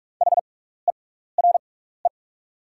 Morse code, heard as a keyed tone, about 44 wpm, spaced as for 15 wpm, 690 Hz. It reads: HERE
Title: 6x Character Spacing - Top 300 Words - 45wpm